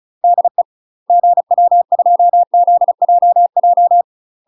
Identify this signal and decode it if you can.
Morse code, heard as a keyed tone, about 34 wpm, 700 Hz. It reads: DE GW2ZJJ
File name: UK MASTER QSOs TRAINING / 35WPM / UK MASTER QSOs 0023